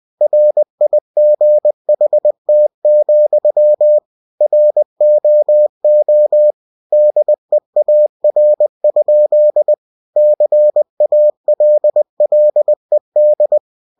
Morse code, heard as a keyed tone, about 20 wpm, 600 Hz. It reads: RIGHT, ROO DEAR? CALLED